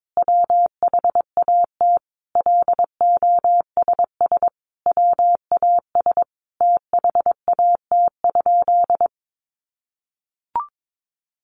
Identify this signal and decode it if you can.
Morse code, heard as a keyed tone, about 22 wpm, 700 Hz. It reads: WHAT LOSS WAS THAT?
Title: Sentences from Top 1400 Words 22wpm